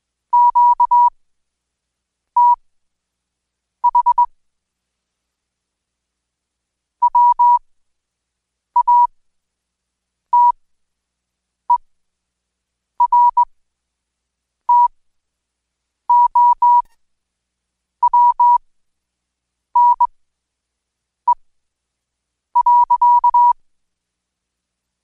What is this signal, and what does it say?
Morse code, heard as a keyed tone, about 20 wpm, spaced as for 6 wpm, 970 Hz. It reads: QTH WATERTOWNE.
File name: Code Quick/AudioDownld2/Seg109-Day14-DayDrill 14